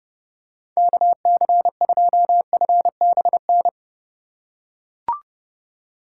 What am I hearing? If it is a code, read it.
Morse code, heard as a keyed tone, about 27 wpm, 700 Hz. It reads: KC2FBN